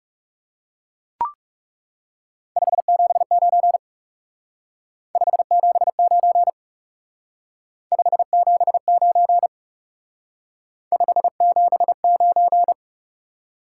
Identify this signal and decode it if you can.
Morse code, heard as a keyed tone, about 38 wpm, 700 Hz. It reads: 579 579 579 579